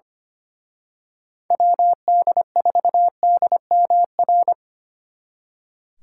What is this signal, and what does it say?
Morse code, heard as a keyed tone, about 29 wpm, 700 Hz. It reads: WD4DMR
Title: Speed Racing - USA 2x3 Call Signs 25wpm